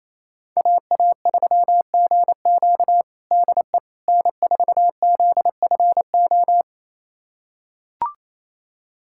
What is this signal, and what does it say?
Morse code, heard as a keyed tone, about 28 wpm, 700 Hz. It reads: AA3GQ DE N4ZFO
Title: USA Callsign DE Callsign - 5 Characters - 28wpm